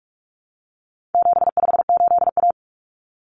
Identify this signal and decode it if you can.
Morse code, heard as a keyed tone, about 44 wpm, 700 Hz. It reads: 758A
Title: Common Cut Numbers - 4-Digit 45wpm